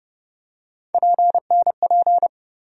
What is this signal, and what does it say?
Morse code, heard as a keyed tone, about 30 wpm, 700 Hz. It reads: PNP
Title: USA Call Sign Postfixes - 3 Letters 30wpm